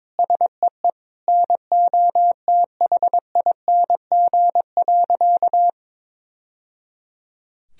Morse code, heard as a keyed tone, about 22 wpm, 700 Hz. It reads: SEE NOTHING.
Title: Sentences from Top 1000 Words - Review Part 2 - 22wpm